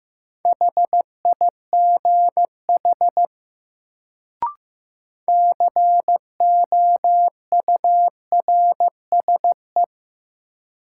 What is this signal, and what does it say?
Morse code, heard as a keyed tone, about 15 wpm, 700 Hz. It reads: HIGH COURSE
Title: Sets of 2 Words - New Word Drill - 15wpm